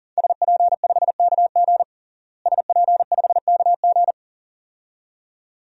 Morse code, heard as a keyed tone, about 37 wpm, 700 Hz. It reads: SP5KG SP5KG
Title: DXCC-100 Call Signs - 5 Characters - Repeated 2x - 40wpm